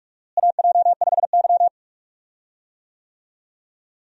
Morse code, heard as a keyed tone, about 42 wpm, 700 Hz. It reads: AJ5Y
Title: USA Call Signs - 4 Characters - Repeated 3x - 45wpm